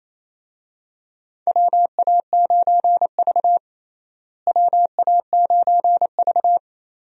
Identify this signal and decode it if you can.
Morse code, heard as a keyed tone, about 28 wpm, 700 Hz. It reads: WA9V WA9V
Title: USA Call Signs - 4 Characters - Repeated 2x - 28wpm